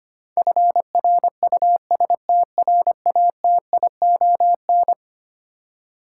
Morse code, heard as a keyed tone, about 25 wpm, 700 Hz. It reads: FRUSTRATION
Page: morse-code-ninja-practice-25wpm